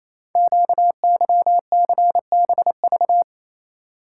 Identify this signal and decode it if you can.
Morse code, heard as a keyed tone, about 27 wpm, 700 Hz. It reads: QYCBV
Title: Code Groups - 5 Letters - 28wpm